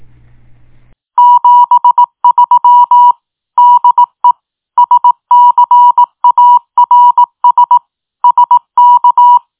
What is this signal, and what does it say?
Morse code, heard as a keyed tone, about 18 wpm, 970 Hz. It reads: 73 DE SCARS SK